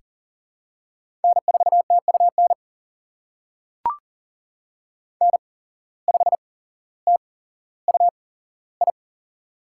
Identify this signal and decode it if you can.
Morse code, heard as a keyed tone, about 40 wpm, spaced as for 17 wpm, 700 Hz. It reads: N4TUN N5TUI
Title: USA Call Signs - 5 Characters - 8x Character Spacing - 40wpm